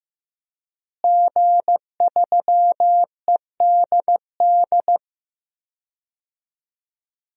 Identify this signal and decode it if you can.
Morse code, heard as a keyed tone, about 15 wpm, 700 Hz. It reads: G3EDD
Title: DXCC-100 Call Signs - 5 Characters - Repeated 3x - 15wpm